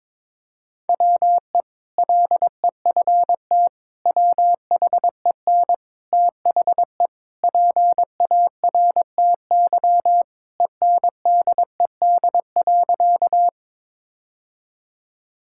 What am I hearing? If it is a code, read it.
Morse code, heard as a keyed tone, about 21 wpm, 700 Hz. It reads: WE LEFT WHEN THE PARTY ENDED.
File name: practice / 22wpm / Sentences with 2-Word Phrases - 22wpm